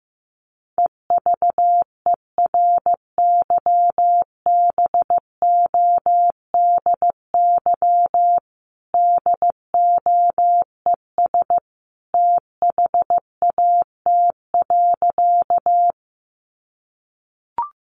Morse code, heard as a keyed tone, about 15 wpm, 700 Hz. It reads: EVERYBODY DOES THAT.
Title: Sentences from Top 1400 Words 15wpm